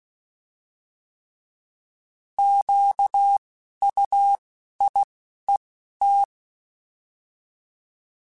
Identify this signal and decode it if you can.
Morse code, heard as a keyed tone, about 16 wpm, 800 Hz. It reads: Q U I E T
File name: Alphabetical / Alphabetical mp3s / Qs @10